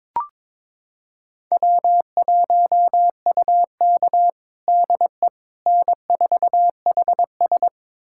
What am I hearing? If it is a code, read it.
Morse code, heard as a keyed tone, about 22 wpm, 700 Hz. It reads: W1UK DE N4HS
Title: USA Callsign DE Callsign - 4 Characters - 22wpm